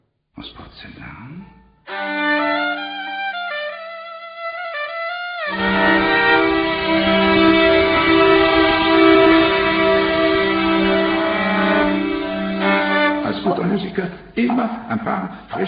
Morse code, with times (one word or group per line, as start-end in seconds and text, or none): none